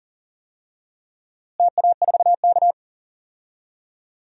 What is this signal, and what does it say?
Morse code, heard as a keyed tone, about 40 wpm, 700 Hz. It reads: TA4K
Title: DXCC-100 Call Signs - 4 Characters - Repeated 2x - 40wpm